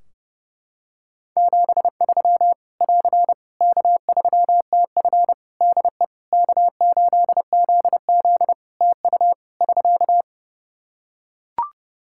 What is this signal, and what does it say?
Morse code, heard as a keyed tone, about 30 wpm, 700 Hz. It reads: 73 <AR> K3TF DE K8ZZ TU <SK>
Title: K1USN SST QSO Practice 30wpm